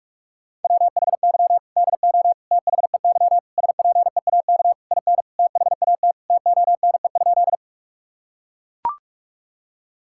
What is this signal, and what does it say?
Morse code, heard as a keyed tone, about 41 wpm, 700 Hz. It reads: WHY DO THEY SPEAK IN THAT TONE?
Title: Sentences from Top 1600 Words 45wpm